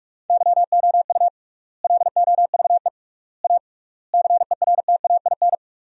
Morse code, heard as a keyed tone, about 47 wpm, 700 Hz. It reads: YOU LOVE A CERTAIN
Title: 3x Word-Spacing - Sentences from Top 500 Words - 45wpm